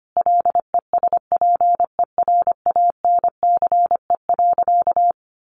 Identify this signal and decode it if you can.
Morse code, heard as a keyed tone, about 25 wpm, 700 Hz. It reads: LESPERANCE.
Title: A Sound Of Thunder 25wpm